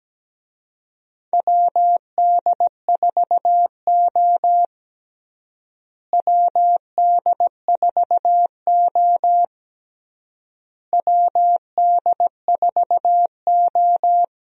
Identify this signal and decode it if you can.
Morse code, heard as a keyed tone, about 17 wpm, 700 Hz. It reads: WD4O WD4O WD4O
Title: USA Call Signs - 4 Characters - Repeated 3x - 17wpm